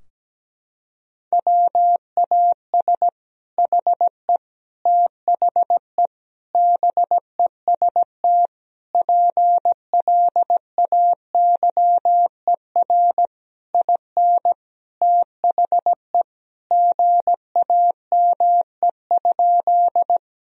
Morse code, heard as a keyed tone, about 17 wpm, 700 Hz. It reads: WAS HE THE BEST PLAYER IN THE GAME?
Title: Sentences from Top 1000 Words - Review Part 4 - 17wpm